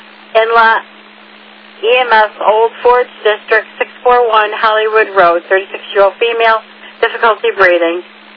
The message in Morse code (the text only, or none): none